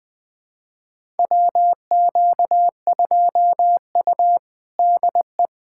Morse code, heard as a keyed tone, about 20 wpm, 700 Hz. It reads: WQ2U DE
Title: USA Callsign DE Callsign - 4 Characters - 20wpm